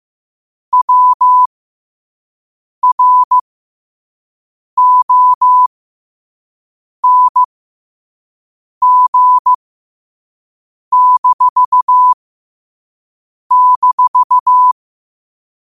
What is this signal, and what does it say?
Morse code, heard as a keyed tone, about 15 wpm, spaced as for 5 wpm, 1000 Hz. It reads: WRONG--
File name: qotdfiles / 5wpm